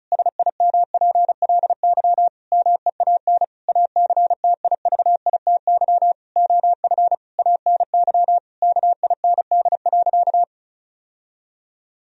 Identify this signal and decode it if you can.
Morse code, heard as a keyed tone, about 34 wpm, 700 Hz. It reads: SIMPLY MEAN ACTIVITY OF ANY KIND.
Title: Sentences from Top 1000 Words - Review Part 4 - 35wpm